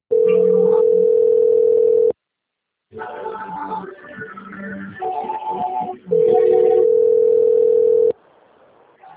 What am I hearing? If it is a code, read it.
Ringback tone (North America), 440 + 480 Hz.
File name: out- - - - - .4037